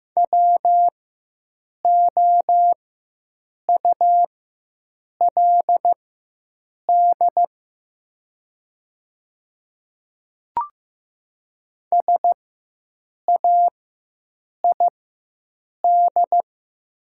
Morse code, heard as a keyed tone, about 15 wpm, spaced as for 7 wpm, 700 Hz. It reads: WOULD SAID
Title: 4x Character Spacing - Top 100 Words - 15wpm